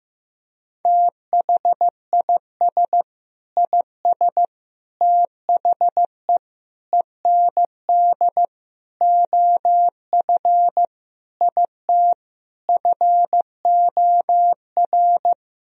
Morse code, heard as a keyed tone, about 15 wpm, 700 Hz. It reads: THIS IS THE END OF IT FOR
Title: Sentences from Top 1000 Words - Review Part 3 - 15wpm